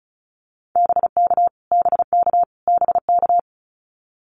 Morse code, heard as a keyed tone, about 34 wpm, 700 Hz. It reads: BK BK BK
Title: Abbreviations Repeated 3x - 35wpm